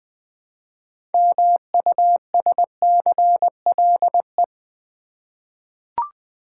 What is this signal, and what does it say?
Morse code, heard as a keyed tone, about 20 wpm, 700 Hz. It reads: MUSCLE
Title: 6 Letter Words - First 3 Letters Given - 20wpm